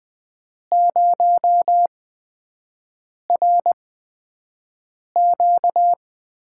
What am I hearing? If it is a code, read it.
Morse code, heard as a keyed tone, about 20 wpm, spaced as for 5 wpm, 700 Hz. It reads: 0RQ
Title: USA Call Signs - 8x Character Spacing - 20wpm